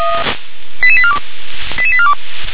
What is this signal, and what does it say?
Five-tone selective call (ZVEI), heard as quick runs of five tones, tones of about 70 ms. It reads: 88842 88842
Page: /357/SDR/ZVEI/88839/2024.02.03/